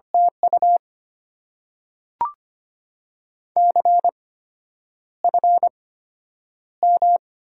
Morse code, heard as a keyed tone, about 25 wpm, spaced as for 9 wpm, 690 Hz. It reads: TU CFM
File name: Top 90 QSO Elements - 8x Character Spacing - 25wpm